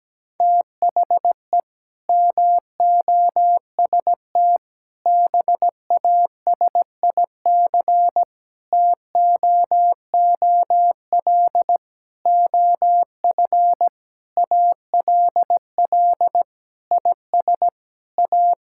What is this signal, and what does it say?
Morse code, heard as a keyed tone, about 17 wpm, 700 Hz. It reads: THE MOST BASIC TOOL OF ALL IS A
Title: Sentences from Top 1000 Words - Review Part 6 - 17wpm